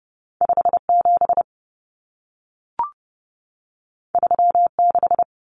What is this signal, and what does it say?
Morse code, heard as a keyed tone, about 30 wpm, 700 Hz. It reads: 57 36
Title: Common Cut Numbers - 2-Digit 30wpm